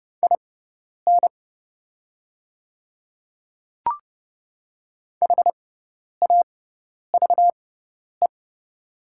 Morse code, heard as a keyed tone, about 31 wpm, spaced as for 10 wpm, 700 Hz. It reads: IN HAVE
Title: 6x Character Spacing - Top 100 Words - 30wpm